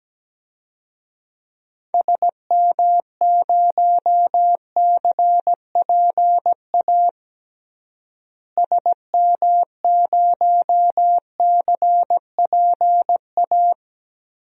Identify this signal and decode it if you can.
Morse code, heard as a keyed tone, about 17 wpm, 700 Hz. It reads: SM0CPA SM0CPA
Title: DXCC-100 Call Signs - 6 Characters - Repeated 3x - 17wpm